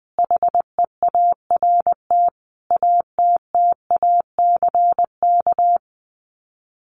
Morse code, heard as a keyed tone, about 20 wpm, 700 Hz. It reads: HEART ATTACK
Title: Sets of 5 Words Encore2 20wpm